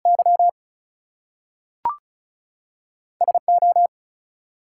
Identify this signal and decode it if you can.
Morse code, heard as a keyed tone, about 36 wpm, 700 Hz. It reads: Y SO
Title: Two Letter Words - 35wpm